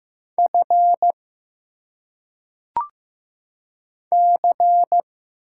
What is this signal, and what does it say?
Morse code, heard as a keyed tone, about 15 wpm, 700 Hz. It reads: F C
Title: Single Letters - Phonetic Pronunciation 15wpm